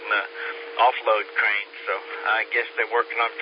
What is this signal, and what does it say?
Emergency Alert System (SAME) tones, heard as harsh buzzing bursts, with speech. It is heard from 0.2 to 2.8 s.